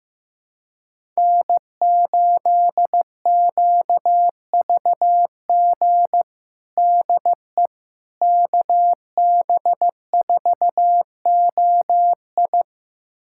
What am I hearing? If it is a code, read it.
Morse code, heard as a keyed tone, about 16 wpm, 700 Hz. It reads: N8QVG DE KB4OI
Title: USA Callsign DE Callsign - 5 Characters - 15wpm